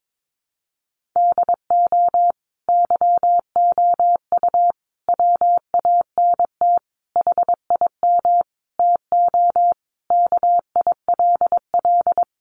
Morse code, heard as a keyed tone, about 22 wpm, 700 Hz. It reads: DO YOU WANT HIM TO KILL
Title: Sentences from Top 1400 Words 22wpm